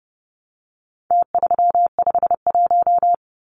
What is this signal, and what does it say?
Morse code, heard as a keyed tone, about 30 wpm, 700 Hz. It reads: T351
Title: Numbers - Contest Style - 4-Digit 30wpm